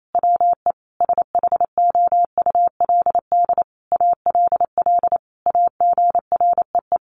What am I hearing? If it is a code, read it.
Morse code, heard as a keyed tone, about 28 wpm, 700 Hz. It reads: WE SHOULD ALL AGREE
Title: Sentences from Top 1100 Words 28wpm